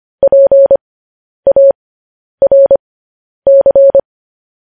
Morse code, heard as a keyed tone, about 25 wpm, spaced as for 10 wpm, 550 Hz. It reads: PARC
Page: news headlines in morse code at 10 wpm